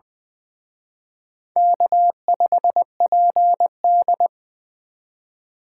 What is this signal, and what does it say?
Morse code, heard as a keyed tone, about 23 wpm, 700 Hz. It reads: K5PD
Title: Speed Racing - USA Call Signs 20wpm